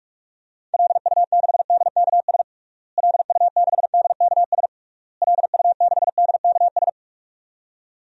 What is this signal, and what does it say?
Morse code, heard as a keyed tone, about 46 wpm, 700 Hz. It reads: LU6DKS LU6DKS LU6DKS
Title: DXCC-100 Call Signs - 6 Characters - Repeated 3x - 45wpm